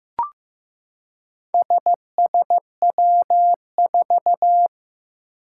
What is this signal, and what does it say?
Morse code, heard as a keyed tone, about 15 wpm, 700 Hz. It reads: SSW4
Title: Four Challenging Letter-Number Combinations 15wpm